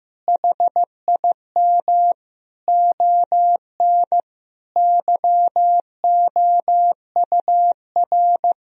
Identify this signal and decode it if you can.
Morse code, heard as a keyed tone, about 15 wpm, 700 Hz. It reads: HIM ON YOUR
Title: Sentences from Top 1000 Words - Review Part 1 - 15wpm